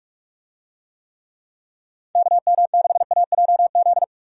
Morse code, heard as a keyed tone, about 46 wpm, 700 Hz. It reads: KM6AJZ
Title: USA Call Signs - 6 Characters - 45wpm